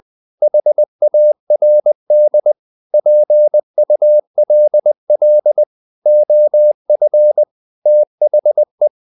Morse code, heard as a keyed tone, about 20 wpm, 600 Hz. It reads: HARD PULL OF THE
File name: the old man and the sea / the old man and the sea part 1 20wpm